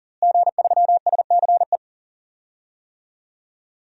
Morse code, heard as a keyed tone, about 38 wpm, 700 Hz. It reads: G3SCE
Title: DXCC-100 Call Signs - 5 Characters - Repeated 2x - 40wpm